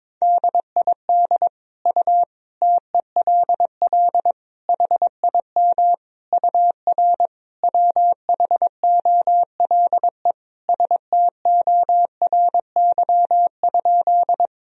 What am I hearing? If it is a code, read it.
Morse code, heard as a keyed tone, about 22 wpm, 700 Hz. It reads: DID U TELL HIM UR WHOLE STORY?
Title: Sentences from Top 400 Words and Abbreviations 22wpm